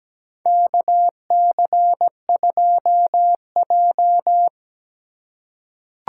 Morse code, heard as a keyed tone, about 17 wpm, 700 Hz. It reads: KC2J
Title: USA Call Signs - 4 Characters - 6x Character Spacing - 17wpm